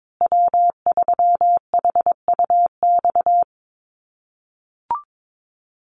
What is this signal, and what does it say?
Morse code, heard as a keyed tone, about 22 wpm, 700 Hz. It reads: W3HUX
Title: USA Call Signs Repeated 2x 22wpm